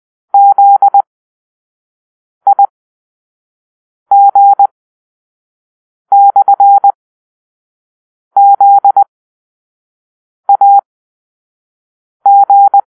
Morse code, heard as a keyed tone, about 20 wpm, spaced as for 5 wpm, 800 Hz. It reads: ZIG/ZAG
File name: Grp-5 Drill-2